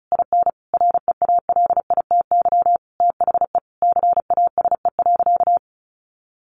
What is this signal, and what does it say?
Morse code, heard as a keyed tone, about 34 wpm, 700 Hz. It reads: IN REALITY THE CASE.
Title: Sentences from Top 1200 Words 35wpm